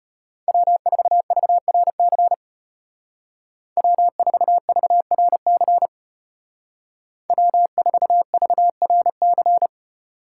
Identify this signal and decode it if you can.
Morse code, heard as a keyed tone, about 33 wpm, 700 Hz. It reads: W4VRC W4VRC W4VRC